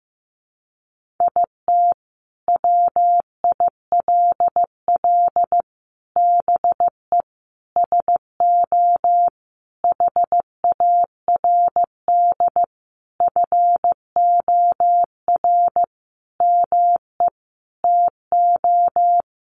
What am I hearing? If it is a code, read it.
Morse code, heard as a keyed tone, about 15 wpm, 700 Hz. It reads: IT WILL BE SO HARD FOR ME TO
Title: Sentences from Top 1500 Words 15wpm